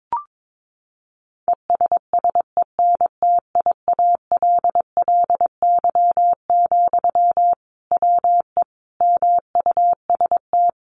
Morse code, heard as a keyed tone, about 22 wpm, 700 Hz. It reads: ESSENTIALLY, WE MUST
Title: Sentences from Top 1300 Words 22wpm